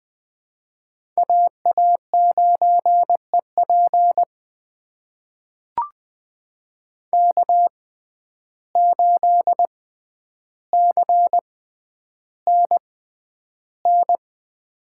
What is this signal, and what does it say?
Morse code, heard as a keyed tone, about 20 wpm, spaced as for 10 wpm, 700 Hz. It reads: AA9EP K8CNN
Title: USA Call Signs - 5 Characters - 6x Character Spacing - 20wpm